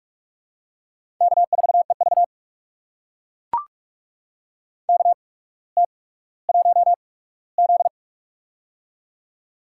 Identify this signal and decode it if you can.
Morse code, heard as a keyed tone, about 44 wpm, spaced as for 19 wpm, 690 Hz. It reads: K4EV KT1Z
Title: USA Call Signs - 4 Characters - 8x Character Spacing - 45wpm